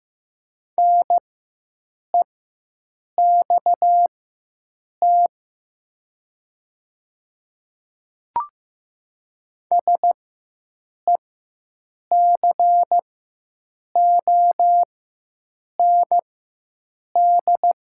Morse code, heard as a keyed tone, about 15 wpm, spaced as for 7 wpm, 700 Hz. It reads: NEXT SECOND
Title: 4x Character Spacing - Top 300 Words - 15wpm